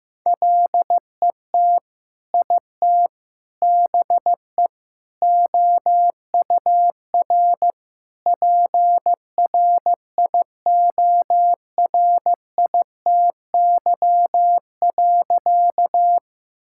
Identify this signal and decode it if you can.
Morse code, heard as a keyed tone, about 15 wpm, 700 Hz. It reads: LET IT BE OUR PRIORITY.
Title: Sentences from Top 1600 Words 15wpm